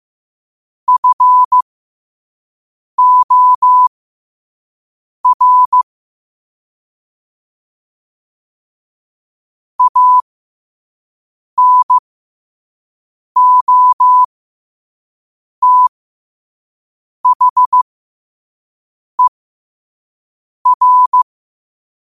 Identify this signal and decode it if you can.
Morse code, heard as a keyed tone, about 15 wpm, spaced as for 5 wpm, 1000 Hz. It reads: FOR ANOTHER
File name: qotdfiles / 5wpm